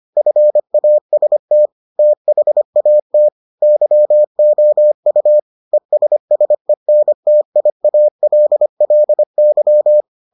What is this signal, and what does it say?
Morse code, heard as a keyed tone, about 25 wpm, 600 Hz. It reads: FAST THAT YOU ESSENTIALLY